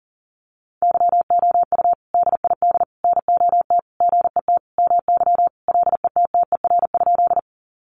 Morse code, heard as a keyed tone, about 40 wpm, 700 Hz. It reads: YOU DID NOT GET MY LETTER?
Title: Sentences from Top 600 Words 40wpm